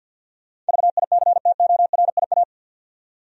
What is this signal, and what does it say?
Morse code, heard as a keyed tone, about 47 wpm, 700 Hz. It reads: VICTORIA
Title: USA Common Names - 50wpm